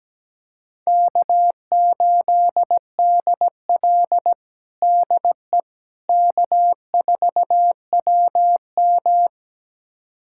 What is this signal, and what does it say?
Morse code, heard as a keyed tone, about 17 wpm, 700 Hz. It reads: K8DL DE K4WM
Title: USA Callsign DE Callsign - 4 Characters - 17wpm